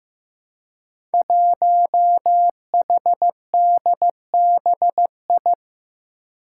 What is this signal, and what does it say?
Morse code, heard as a keyed tone, about 15 wpm, 700 Hz. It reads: 1HDBI
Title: Five Challenging Letter-Number Combinations 15wpm